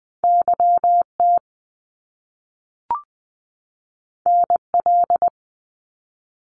Morse code, heard as a keyed tone, about 20 wpm, 700 Hz. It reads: YT NL
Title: Canadian Provinces and Territories 20wpm